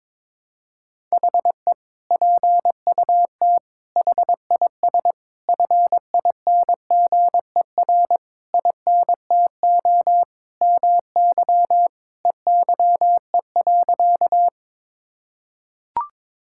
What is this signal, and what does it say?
Morse code, heard as a keyed tone, about 22 wpm, 700 Hz. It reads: HE PUT HIS FINGER INTO MY EYE.
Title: Sentences from Top 1600 Words 22wpm